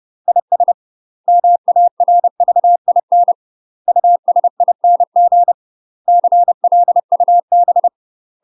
Morse code, heard as a keyed tone, about 30 wpm, 700 Hz. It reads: IS MARVIN USING CLUB
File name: UK MASTER QSOs TRAINING / 30WPM / UK MASTER QSOs 0042